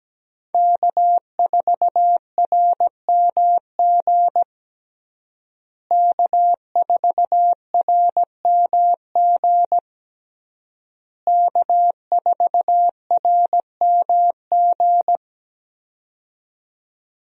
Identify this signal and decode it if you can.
Morse code, heard as a keyed tone, about 17 wpm, 700 Hz. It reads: K4RMG K4RMG K4RMG
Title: USA Call Signs - 5 Characters - Repeated 3x - 17wpm